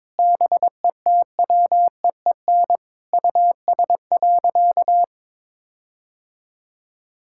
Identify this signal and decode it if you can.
Morse code, heard as a keyed tone, about 22 wpm, 700 Hz. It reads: BETWEEN US.
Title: Sentences from Top 1600 Words 22wpm